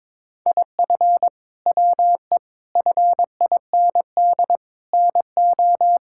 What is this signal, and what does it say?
Morse code, heard as a keyed tone, about 22 wpm, 700 Hz. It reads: IF WE FIND NO
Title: Sentences from Top 1600 Words 22wpm